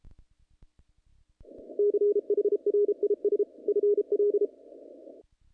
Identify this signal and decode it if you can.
Morse code, heard as a keyed tone, about 33 wpm, 410 Hz. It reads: CHRIS FL
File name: contests / naqp-cw-k5tr-op-k5pi- -2100